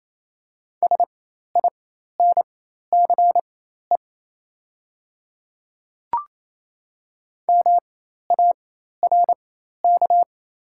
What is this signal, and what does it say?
Morse code, heard as a keyed tone, about 28 wpm, spaced as for 13 wpm, 700 Hz. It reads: SINCE MARK
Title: 4x Character Spacing - Top 300 Words - 28wpm